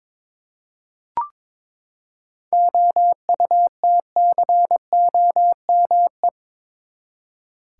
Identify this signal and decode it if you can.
Morse code, heard as a keyed tone, about 22 wpm, 700 Hz. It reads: OUTCOME
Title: Seven Letter Words - 22wpm